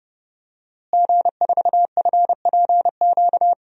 Morse code, heard as a keyed tone, about 30 wpm, 700 Hz. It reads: G4FPQ
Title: DXCC-100 Call Signs - 5 Characters - Repeated 3x - 30wpm